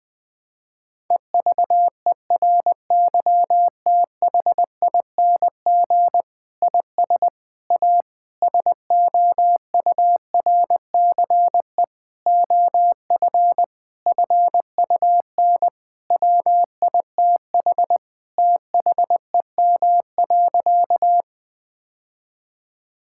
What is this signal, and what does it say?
Morse code, heard as a keyed tone, about 21 wpm, 700 Hz. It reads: EVERYTHING IS A SOURCE OF FUN WITH THEM.
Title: Sentences from Top 1000 Words - Review Part 5 - 20wpm